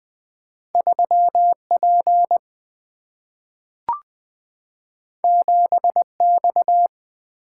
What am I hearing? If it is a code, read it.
Morse code, heard as a keyed tone, about 20 wpm, 700 Hz. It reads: 3P 7X
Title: Single-Letter Single-Number 20wpm